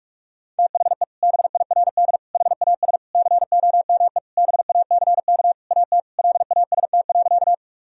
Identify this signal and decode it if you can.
Morse code, heard as a keyed tone, about 46 wpm, 700 Hz. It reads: THE BIRD HAS COME BACK AT LAST.
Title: Sentences from Top 1000 Words - Review Part 2 - 45wpm